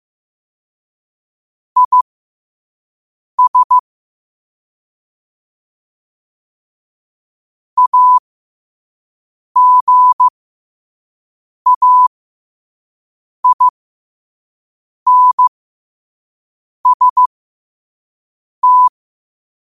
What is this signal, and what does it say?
Morse code, heard as a keyed tone, about 15 wpm, spaced as for 5 wpm, 1000 Hz. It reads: IS AGAINST
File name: qotdfiles / 5wpm